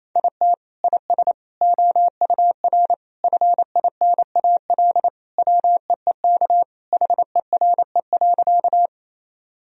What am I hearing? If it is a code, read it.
Morse code, heard as a keyed tone, about 28 wpm, 700 Hz. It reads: IT IS OUR FINAL WEEK HERE.
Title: Sentences from Top 1000 Words - Review Part 2 - 28wpm